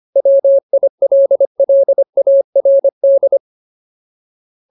Morse code, heard as a keyed tone, about 25 wpm, 550 Hz. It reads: WILLARD
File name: Ultimate Word List / 25WPM Ultimate